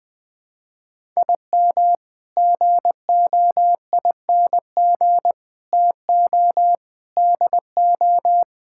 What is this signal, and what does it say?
Morse code, heard as a keyed tone, about 21 wpm, 700 Hz. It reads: IM GOING TO DO
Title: Sets of 8 Words - 20wpm